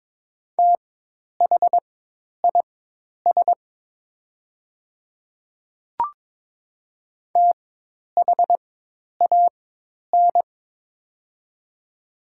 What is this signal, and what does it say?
Morse code, heard as a keyed tone, about 22 wpm, spaced as for 10 wpm, 700 Hz. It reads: THIS THAN
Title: 4x Character Spacing - Top 100 Words - 22wpm